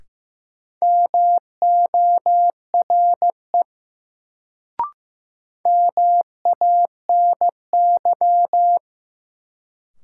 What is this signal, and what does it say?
Morse code, heard as a keyed tone, about 15 wpm, 700 Hz. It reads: MORE MANY
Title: Sets of 2 Words - New Word Drill - 15wpm